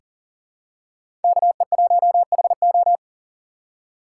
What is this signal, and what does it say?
Morse code, heard as a keyed tone, about 40 wpm, 700 Hz. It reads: KE1HO
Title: USA Call Signs - 5 Characters - 3x Character Spacing - 40wpm